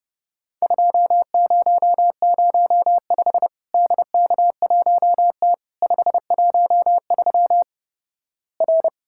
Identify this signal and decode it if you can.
Morse code, heard as a keyed tone, about 30 wpm, 700 Hz. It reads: 2005 DK1T 513 R
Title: Worked All Europe DX Contest - QTC Exchange - 30wpm